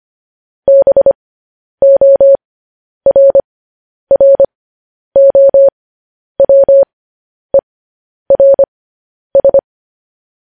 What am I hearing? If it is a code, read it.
Morse code, heard as a keyed tone, about 25 wpm, spaced as for 10 wpm, 550 Hz. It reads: BORROWERS
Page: news headlines in morse code at 10 wpm